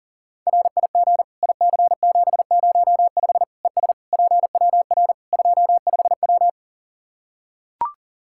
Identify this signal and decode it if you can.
Morse code, heard as a keyed tone, about 40 wpm, 700 Hz. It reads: RIG IC705 ES PWR 25W
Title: Practice QSO - Exchange 2 - Info - 40wpm